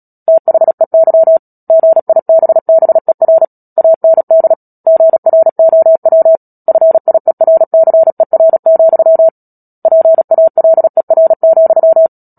Morse code, heard as a keyed tone, about 37 wpm, 650 Hz. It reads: THEY GIBBER AND GROW FIERCER, PALER,
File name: Book Two / 10-Chapter 10 - Epilogue - 36 WPM